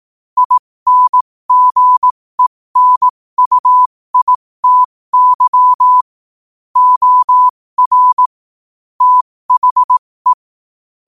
Morse code, heard as a keyed tone, about 18 wpm, spaced as for 15 wpm, 1000 Hz. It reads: INGENUITY OR THE
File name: qotdfiles / 15wpm